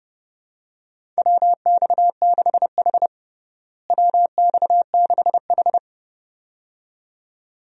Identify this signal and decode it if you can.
Morse code, heard as a keyed tone, about 30 wpm, 700 Hz. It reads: WX6H WX6H
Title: USA Call Signs - 4 Characters - Repeated 2x - 30wpm